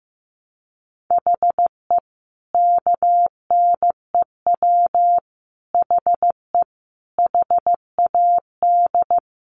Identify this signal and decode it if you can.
Morse code, heard as a keyed tone, about 15 wpm, 700 Hz. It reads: HE KNEW HE HAD
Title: Sentences from Top 600 Words 15wpm